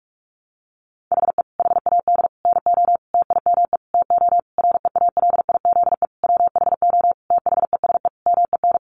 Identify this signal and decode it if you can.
Morse code, heard as a keyed tone, about 44 wpm, 700 Hz. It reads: HE HAD NO TIME TO REALIZE WHO THESE MEN
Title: Sentences from Top 1000 Words 45wpm